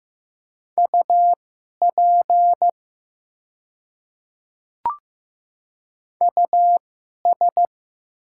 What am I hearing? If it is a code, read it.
Morse code, heard as a keyed tone, about 15 wpm, spaced as for 11 wpm, 700 Hz. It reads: UP US
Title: 2x Character Spacing - Top 200 Words - 15wpm